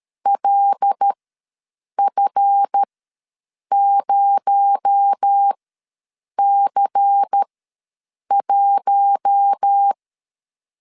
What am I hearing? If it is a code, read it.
Morse code, heard as a keyed tone, about 13 wpm, spaced as for 7 wpm, 800 Hz. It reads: LF0C1
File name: LEZIONI / 7 GF9Z / 7 GF ws 12wpm 3140